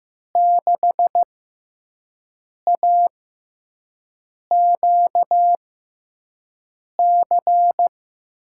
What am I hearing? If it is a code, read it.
Morse code, heard as a keyed tone, about 15 wpm, spaced as for 5 wpm, 700 Hz. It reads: 6AQC
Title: USA Call Signs - 6 Characters - 6x Character Spacing - 15wpm